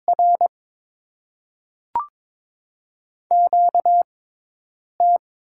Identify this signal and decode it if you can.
Morse code, heard as a keyed tone, about 22 wpm, 700 Hz. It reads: R Q T
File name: Top 20 QSO Elements - 6x Character Spacing - 22wpm